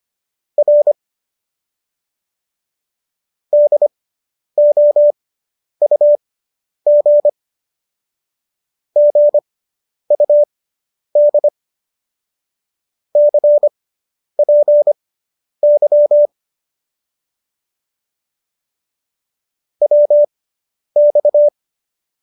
Morse code, heard as a keyed tone, about 25 wpm, spaced as for 10 wpm, 590 Hz. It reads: R DOUG GUD CPY WX
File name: QSO220 10